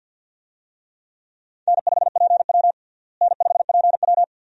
Morse code, heard as a keyed tone, about 53 wpm, 700 Hz. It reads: N5PW N5PW
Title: USA Call Signs - 4 Characters - Repeated 2x - 50wpm